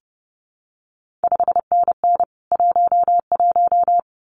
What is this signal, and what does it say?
Morse code, heard as a keyed tone, about 30 wpm, 700 Hz. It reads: 5NN 11
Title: CQ WW DX Contest Exchange Encore - 30wpm